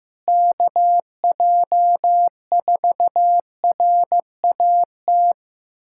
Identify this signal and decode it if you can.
Morse code, heard as a keyed tone, about 15 wpm, 700 Hz. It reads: KJ4RAT
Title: USA Callsign DE Callsign 15wpm